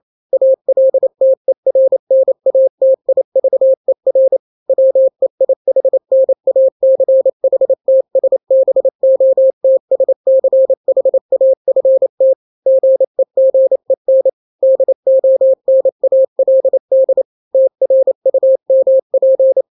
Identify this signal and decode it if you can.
Morse code, heard as a keyed tone, about 28 wpm, 530 Hz. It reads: ALTERNATIVER WEIHNACHTSBOTSCHAFT GEGEN DONALD TRUMP
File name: daily / 27wpm